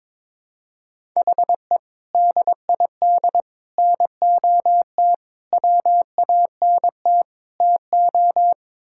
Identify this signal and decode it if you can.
Morse code, heard as a keyed tone, about 22 wpm, 700 Hz. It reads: HE DID NOT WANT TO
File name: practice / 22wpm / Build A Phrase - 5-Word Phrases - 22wpm